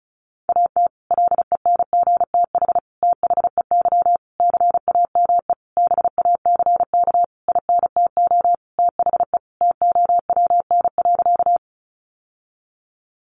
Morse code, heard as a keyed tone, about 34 wpm, 700 Hz. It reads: AT LENGTH THEY CAME BACK INTO THE TOWN.
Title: Sentences from Top 1000 Words 35wpm